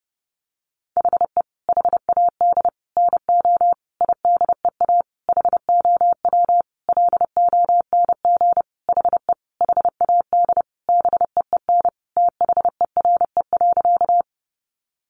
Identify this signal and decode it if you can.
Morse code, heard as a keyed tone, about 28 wpm, 700 Hz. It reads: HE HAD NO IDEA HOW LONG HE HAD BEEN THERE.
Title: Sentences from Top 300 Words 30wpm